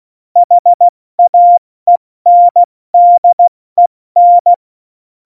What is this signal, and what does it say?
Morse code, heard as a keyed tone, about 16 wpm, 700 Hz. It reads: HAENDEN
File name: Morsekurs / Witze / Witz 12. Februar 2019, 70 F80